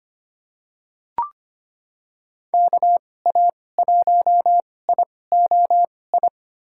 Morse code, heard as a keyed tone, about 25 wpm, spaced as for 18 wpm, 690 Hz. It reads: KA1IOI
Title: USA Call Signs - 6 Characters - 2x Character Spacing - 25wpm